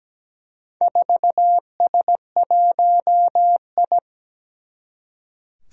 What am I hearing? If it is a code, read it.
Morse code, heard as a keyed tone, about 17 wpm, 700 Hz. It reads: 4S1I
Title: Four Challenging Letter-Number Combinations 17wpm